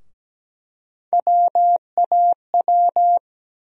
Morse code, heard as a keyed tone, about 17 wpm, 700 Hz. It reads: WAW
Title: USA Call Sign Postfixes - 3 Letters 17wpm